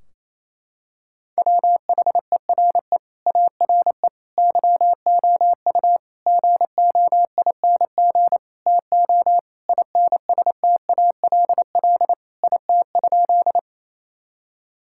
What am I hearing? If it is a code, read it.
Morse code, heard as a keyed tone, about 28 wpm, 700 Hz. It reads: WHERE ARE YOU GOING TO INSTALL IT?
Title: Sentences from Top 1600 Words 28wpm